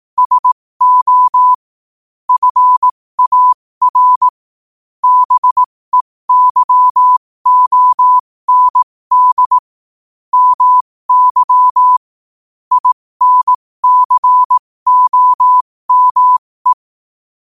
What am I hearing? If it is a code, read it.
Morse code, heard as a keyed tone, about 18 wpm, spaced as for 15 wpm, 1000 Hz. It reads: SO FAR BEYOND MY INCOME
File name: qotdfiles / 15wpm